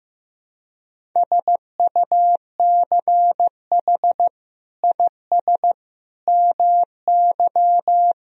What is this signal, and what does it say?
Morse code, heard as a keyed tone, about 15 wpm, 700 Hz. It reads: SUCH IS MY
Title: Sentences from Top 1000 Words - Review Part 1 - 15wpm